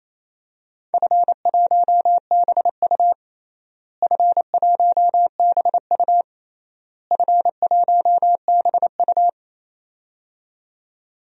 Morse code, heard as a keyed tone, about 28 wpm, 700 Hz. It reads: F1BU F1BU F1BU
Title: DXCC-100 Call Signs - 4 Characters - Repeated 3x - 28wpm